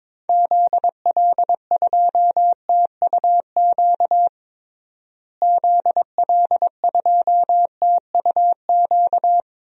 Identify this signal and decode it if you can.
Morse code, heard as a keyed tone, about 22 wpm, 700 Hz. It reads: ZL2TUQ ZL2TUQ
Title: DXCC-100 Call Signs - 6 Characters - Repeated 2x - 22wpm